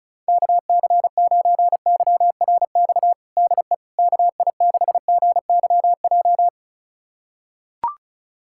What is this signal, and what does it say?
Morse code, heard as a keyed tone, about 36 wpm, 700 Hz. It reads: KC9YRX DE KI6GYJ
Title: USA Callsign DE Callsign 35wpm